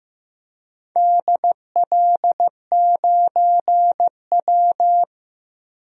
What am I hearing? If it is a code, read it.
Morse code, heard as a keyed tone, about 16 wpm, 700 Hz. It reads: DL9W
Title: DXCC-100 Call Signs - 4 Characters - 15wpm